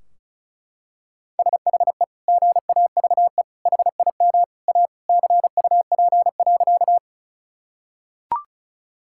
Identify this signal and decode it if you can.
Morse code, heard as a keyed tone, about 36 wpm, 700 Hz. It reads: SHE GAVE HIM A CUP.
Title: Sentences from Top 1600 Words 35wpm